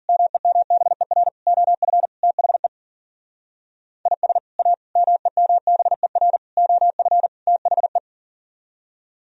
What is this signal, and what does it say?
Morse code, heard as a keyed tone, about 52 wpm, 700 Hz. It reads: MEMBER OF THE IS A MEMBER OF THE